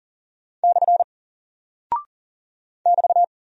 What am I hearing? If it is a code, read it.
Morse code, heard as a keyed tone, about 40 wpm, 700 Hz. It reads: / <BT>
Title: Single Character (Letter:Number:Punctuation:Prosign) 40wpm